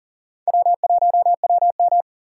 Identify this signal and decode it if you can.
Morse code, heard as a keyed tone, about 40 wpm, 700 Hz. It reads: W1WM
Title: USA Call Signs - 4 Characters - 3x Character Spacing - 40wpm